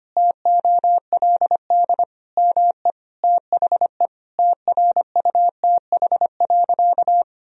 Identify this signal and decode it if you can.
Morse code, heard as a keyed tone, about 25 wpm, 700 Hz. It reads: TOLD ME THE TRUTH.
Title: Sentences from Top 1000 Words - Review Part 4 - 25wpm